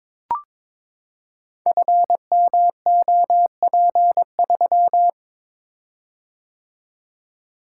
Morse code, heard as a keyed tone, about 22 wpm, 700 Hz. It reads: FMOP3
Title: Code Groups - 5 Letter:Numbers - 22wpm